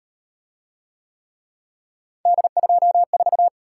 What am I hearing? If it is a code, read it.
Morse code, heard as a keyed tone, about 38 wpm, 700 Hz. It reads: D24